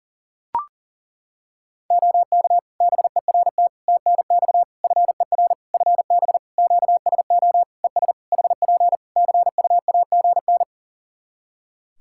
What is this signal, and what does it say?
Morse code, heard as a keyed tone, about 37 wpm, 700 Hz. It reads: OK BERT TNX FER FB QSO ES HP CUAGN
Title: Practice QSO - Exchange 4 - Ending - 40wpm